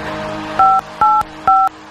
DTMF, heard as telephone touch tones, tones of about 220 ms. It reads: 585